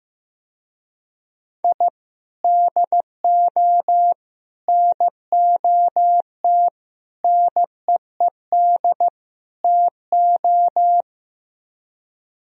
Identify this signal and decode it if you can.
Morse code, heard as a keyed tone, about 15 wpm, 700 Hz. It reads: I DO NOT NEED TO
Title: Speed-Racing - Sets of 5 Words 15wpm